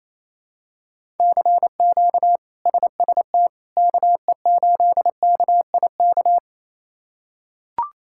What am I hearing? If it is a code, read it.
Morse code, heard as a keyed tone, about 28 wpm, 700 Hz. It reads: CQ SST KE8KIK
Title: K1USN SST QSO Practice - 28wpm